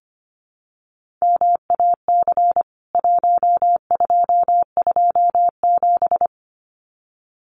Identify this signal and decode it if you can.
Morse code, heard as a keyed tone, about 25 wpm, 700 Hz. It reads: MAC 1227
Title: CWT Exchange 25wpm